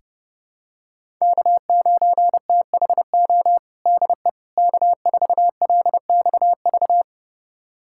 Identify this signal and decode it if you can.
Morse code, heard as a keyed tone, about 30 wpm, 700 Hz. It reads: K9THO DE K4LXV
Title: USA Callsign DE Callsign 30wpm